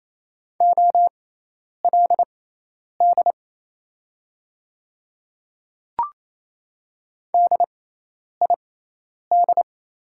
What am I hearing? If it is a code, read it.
Morse code, heard as a keyed tone, about 28 wpm, spaced as for 10 wpm, 700 Hz. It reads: OLD DID
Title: 6x Character Spacing - Top 200 Words - 28wpm